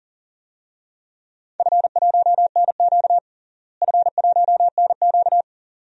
Morse code, heard as a keyed tone, about 40 wpm, 700 Hz. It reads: F1NQ F1NQ
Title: DXCC-100 Call Signs - 4 Characters - Repeated 2x - 40wpm